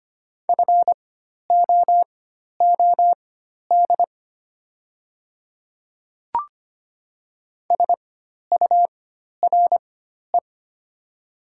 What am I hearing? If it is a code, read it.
Morse code, heard as a keyed tone, about 25 wpm, spaced as for 12 wpm, 700 Hz. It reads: FOOD SURE
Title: 4x Character Spacing - Top 300 Words - 25wpm